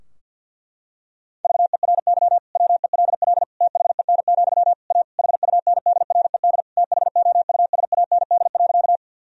Morse code, heard as a keyed tone, about 52 wpm, 700 Hz. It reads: VERY WELL THEN, A HUNDRED THOUSAND.
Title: Sentences from Top 1000 Words - Review Part 3 - 50wpm